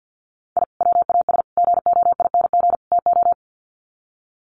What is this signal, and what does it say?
Morse code, heard as a keyed tone, about 50 wpm, 700 Hz. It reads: I WAS GOING TO
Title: Sets of 4 Words 50wpm